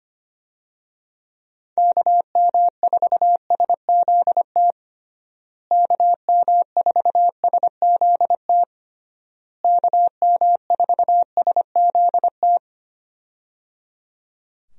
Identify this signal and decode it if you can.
Morse code, heard as a keyed tone, about 25 wpm, 700 Hz. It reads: KM4SZT KM4SZT KM4SZT
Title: USA Call Signs - 6 Characters - Repeated 3x - 25wpm